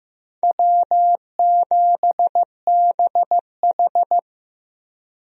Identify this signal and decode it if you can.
Morse code, heard as a keyed tone, about 15 wpm, 700 Hz. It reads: W7BH
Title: USA Call Signs - 4 Characters - Repeated 2x - 15wpm